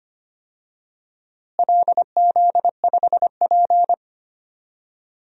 Morse code, heard as a keyed tone, about 25 wpm, 700 Hz. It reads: LZ5P
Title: DXCC-100 Call Signs - 4 Characters - Repeated 2x - 25wpm